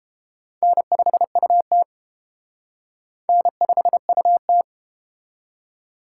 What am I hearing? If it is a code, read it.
Morse code, heard as a keyed tone, about 30 wpm, 700 Hz. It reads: N5UT N5UT